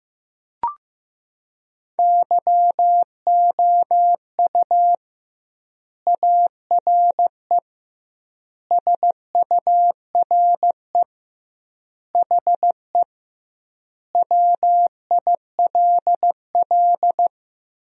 Morse code, heard as a keyed tone, about 15 wpm, 700 Hz. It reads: YOU ARE SURE HE WILL
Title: 2x Word-Spacing - Sentences from Top 300 Words - 15wpm